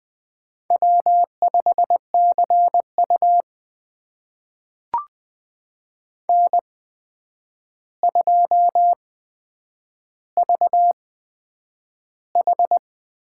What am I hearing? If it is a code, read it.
Morse code, heard as a keyed tone, about 20 wpm, spaced as for 9 wpm, 700 Hz. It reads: W5CU N2VH
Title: USA Call Signs - 4 Characters - 8x Character Spacing - 20wpm